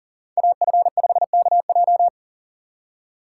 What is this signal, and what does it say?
Morse code, heard as a keyed tone, about 40 wpm, 700 Hz. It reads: AF5KJ
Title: USA Call Signs - 5 Characters - 2x Character Spacing - 40wpm